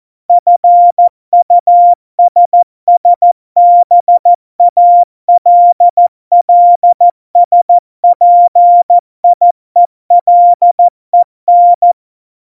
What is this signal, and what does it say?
Morse code, heard as a keyed tone, about 14 wpm, 700 Hz. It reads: FUSSBALLSPIELEN
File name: Morsekurs / Witze / MP3, Witz 10 April 2018, 70 F80